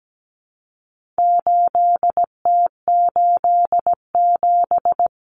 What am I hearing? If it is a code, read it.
Morse code, heard as a keyed tone, about 17 wpm, 700 Hz. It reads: 8T87
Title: Common Cut Numbers - 4-Digit 17wpm